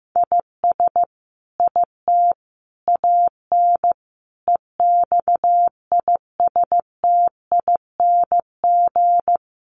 Morse code, heard as a keyed tone, about 15 wpm, 700 Hz. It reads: IS IT AN EXISTING
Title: Sentences from Top 1500 Words 15wpm